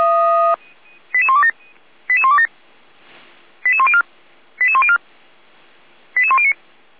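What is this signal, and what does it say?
Five-tone selective call (ZVEI), heard as quick runs of five tones, tones of about 70 ms. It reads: 88127 88127 88104 88104 88109
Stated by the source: Five-tone ZVEI selective call 88192 received by SDR, from feuerwehr-orlamuende fire brigade server.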